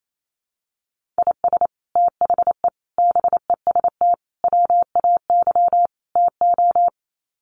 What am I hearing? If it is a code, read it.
Morse code, heard as a keyed tone, about 28 wpm, 700 Hz. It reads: IS THE BEST WAY TO
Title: Sets of 5 Words Encore 28wpm